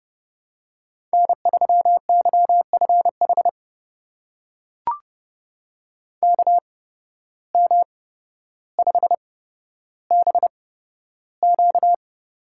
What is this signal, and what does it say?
Morse code, heard as a keyed tone, about 30 wpm, spaced as for 13 wpm, 700 Hz. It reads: N3YFH KM5BQ
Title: USA Call Signs - 5 Characters - 8x Character Spacing - 30wpm